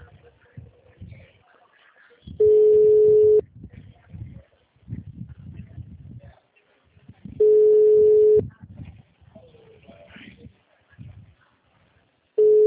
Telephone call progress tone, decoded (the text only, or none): ringback tone